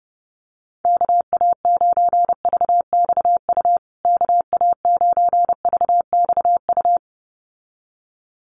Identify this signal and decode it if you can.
Morse code, heard as a keyed tone, about 30 wpm, 700 Hz. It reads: KA9VXU KA9VXU
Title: USA Call Signs Repeated 2x 30wpm